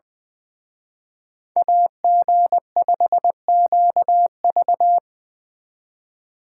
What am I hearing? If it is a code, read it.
Morse code, heard as a keyed tone, about 24 wpm, 700 Hz. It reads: AG5QV
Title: Speed Racing - USA Call Signs 20wpm